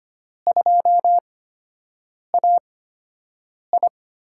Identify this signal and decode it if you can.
Morse code, heard as a keyed tone, about 25 wpm, spaced as for 7 wpm, 700 Hz. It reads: 2AI
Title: USA Call Signs - 5 Characters - 8x Character Spacing - 25wpm